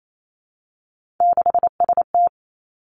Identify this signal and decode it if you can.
Morse code, heard as a keyed tone, about 28 wpm, 700 Hz. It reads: 6ST
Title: Three Challenging Letter-Number Combinations 28wpm